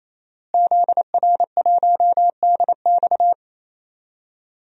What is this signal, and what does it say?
Morse code, heard as a keyed tone, about 28 wpm, 700 Hz. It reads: ZR1DX
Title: DXCC-100 Call Signs - 5 Characters - Repeated 3x - 28wpm